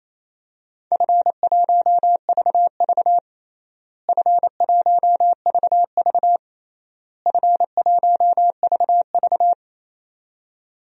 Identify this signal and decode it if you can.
Morse code, heard as a keyed tone, about 28 wpm, 700 Hz. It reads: F1VV F1VV F1VV
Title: DXCC-100 Call Signs - 4 Characters - Repeated 3x - 28wpm